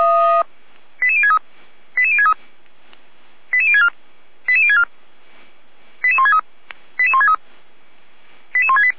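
Five-tone selective call (ZVEI), heard as quick runs of five tones, tones of about 70 ms. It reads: 88063 88063 88064 88064 88163 88163 88168
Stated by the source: Five-tone ZVEI selective call received on SDR.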